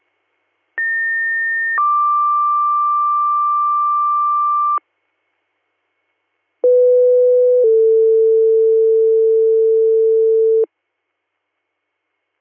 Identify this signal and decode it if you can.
Two-tone page, heard as a one-second tone, then a three-second tone, about 1800 then 1180 Hz, about 500 then 450 Hz. It is heard from 0.8 to 10.6 s.